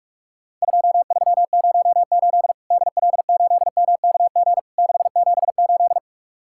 Morse code, heard as a keyed tone, about 46 wpm, 700 Hz. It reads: 2308 DL8MKG 678
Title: Worked All Europe DX Contest - QTC Exchange - 45wpm